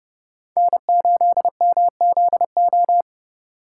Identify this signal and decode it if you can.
Morse code, heard as a keyed tone, about 35 wpm, 700 Hz. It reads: N8MZO